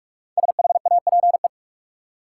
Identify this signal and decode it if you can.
Morse code, heard as a keyed tone, about 46 wpm, 700 Hz. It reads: SHAPE
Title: Five Letter Words - 45wpm